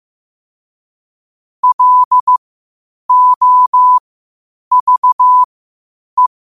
Morse code, heard as a keyed tone, about 15 wpm, spaced as for 8 wpm, 1000 Hz. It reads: LOVE
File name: qotdfiles / 8wpm